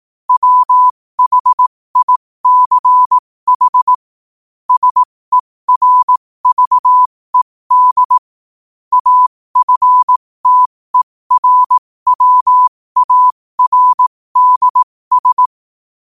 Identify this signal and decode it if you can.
Morse code, heard as a keyed tone, about 18 wpm, spaced as for 15 wpm, 1000 Hz. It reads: WHICH SERVED AFTERWARDS
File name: 15wpm